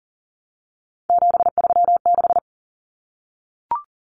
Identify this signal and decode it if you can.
Morse code, heard as a keyed tone, about 40 wpm, 700 Hz. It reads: 736
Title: Common Cut Numbers - 3-Digit 40wpm